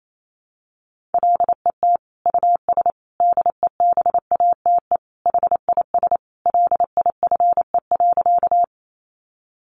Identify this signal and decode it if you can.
Morse code, heard as a keyed tone, about 28 wpm, 700 Hz. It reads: LET US DEBATE HIS LIFE.
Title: Sentences from Top 1100 Words 28wpm